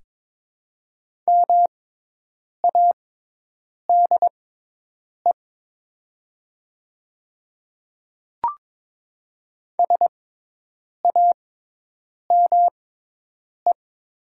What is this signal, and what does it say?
Morse code, heard as a keyed tone, about 22 wpm, spaced as for 8 wpm, 700 Hz. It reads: MADE SAME
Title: 6x Character Spacing - Top 300 Words - 22wpm